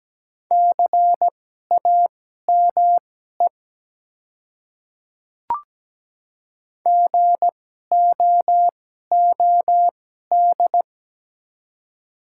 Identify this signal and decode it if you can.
Morse code, heard as a keyed tone, about 17 wpm, spaced as for 12 wpm, 700 Hz. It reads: CAME GOOD
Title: 2x Character Spacing - Top 200 Words - 17wpm